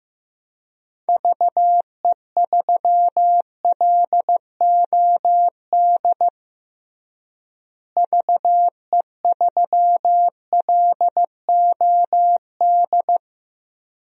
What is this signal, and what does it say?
Morse code, heard as a keyed tone, about 16 wpm, 700 Hz. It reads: VE3LOD VE3LOD
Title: DXCC-100 Call Signs - 6 Characters - Repeated 2x - 15wpm